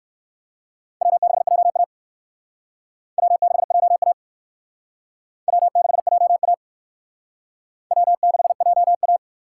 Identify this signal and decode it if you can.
Morse code, heard as a keyed tone, about 58 wpm, 700 Hz. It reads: W6JA W6JA W6JA W6JA